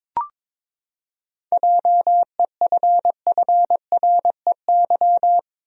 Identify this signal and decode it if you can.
Morse code, heard as a keyed tone, about 23 wpm, 700 Hz. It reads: JEFFREY
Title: USA Common Names - 22wpm